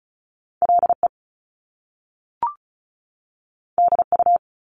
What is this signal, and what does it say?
Morse code, heard as a keyed tone, about 36 wpm, 700 Hz. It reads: LE DU
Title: Top Two Letter Combinations in Words 35wpm